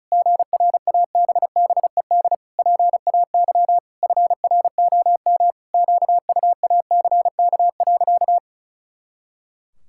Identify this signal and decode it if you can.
Morse code, heard as a keyed tone, about 36 wpm, 700 Hz. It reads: GRABBED PAY FROM QUACK.
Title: Pangram Challenge v2 - 35wpm